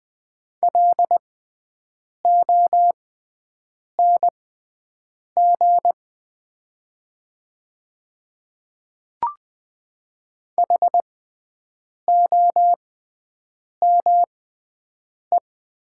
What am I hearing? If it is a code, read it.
Morse code, heard as a keyed tone, about 20 wpm, spaced as for 7 wpm, 700 Hz. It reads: LONG HOME
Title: 6x Character Spacing - Top 200 Words - 20wpm